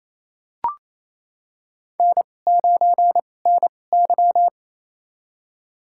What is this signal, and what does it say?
Morse code, heard as a keyed tone, about 28 wpm, spaced as for 20 wpm, 700 Hz. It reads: N9NY
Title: USA Call Signs - 2x Character Spacing - 28wpm